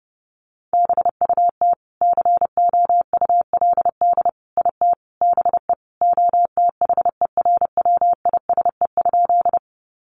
Morse code, heard as a keyed tone, about 30 wpm, 700 Hz. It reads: BUT COULD IT BE OTHERWISE?
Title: Sentences from Top 1200 Words 30wpm